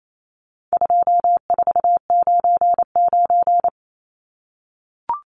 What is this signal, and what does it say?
Morse code, heard as a keyed tone, about 28 wpm, 700 Hz. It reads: 2499
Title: Numbers - 4-Digit 28wpm